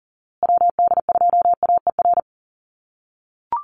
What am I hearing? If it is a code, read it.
Morse code, heard as a keyed tone, about 40 wpm, 700 Hz. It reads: WD2AER
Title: USA Call Signs Repeated 3x 40wpm